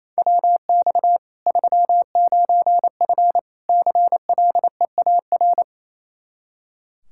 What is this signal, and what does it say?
Morse code, heard as a keyed tone, about 28 wpm, 700 Hz. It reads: WX 39F CLEAR
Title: Practice QSO - Exchange 2 - Info - 28wpm